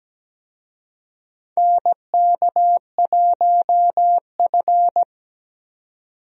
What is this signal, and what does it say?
Morse code, heard as a keyed tone, about 17 wpm, 700 Hz. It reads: NK1F
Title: USA Call Signs - 4 Characters - 17wpm